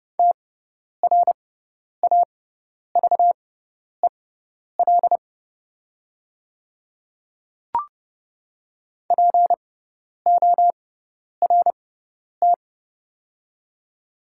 Morse code, heard as a keyed tone, about 30 wpm, spaced as for 10 wpm, 700 Hz. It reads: TRAVEL PORT
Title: 6x Character Spacing - Top 500 Words - 30wpm